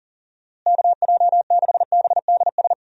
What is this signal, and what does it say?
Morse code, heard as a keyed tone, about 37 wpm, 700 Hz. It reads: KJ6BDS
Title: USA Call Signs - 6 Characters - Repeated 2x - 40wpm